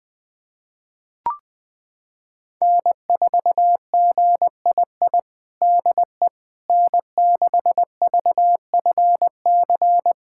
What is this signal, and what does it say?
Morse code, heard as a keyed tone, about 20 wpm, 700 Hz. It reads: N4GII DE N6VFC
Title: USA Callsign DE Callsign - 5 Characters - 20wpm